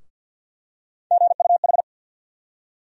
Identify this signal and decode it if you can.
Morse code, heard as a keyed tone, about 52 wpm, 700 Hz. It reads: GUH
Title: USA Call Sign Postfixes - 3 Letters 50wpm